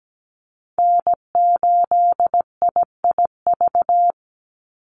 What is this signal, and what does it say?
Morse code, heard as a keyed tone, about 17 wpm, 700 Hz. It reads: N8IIV
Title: USA Call Signs 17wpm